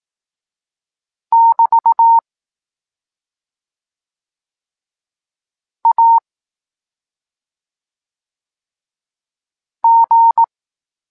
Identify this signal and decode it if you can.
Morse code, heard as a keyed tone, about 18 wpm, 920 Hz. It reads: <BT> A G
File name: Tekst-23 4